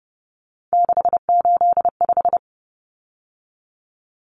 Morse code, heard as a keyed tone, about 30 wpm, 700 Hz. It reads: 685
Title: Numbers - 3-Digit 30wpm